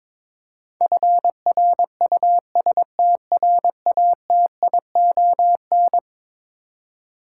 Morse code, heard as a keyed tone, about 22 wpm, 700 Hz. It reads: FRUSTRATION
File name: practice / 22wpm / Eleven Letter Words - 22wpm